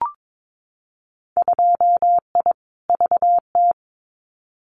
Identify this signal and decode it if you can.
Morse code, heard as a keyed tone, about 22 wpm, 700 Hz. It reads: 2I VT
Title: Winter Field Day Exchanges 22wpm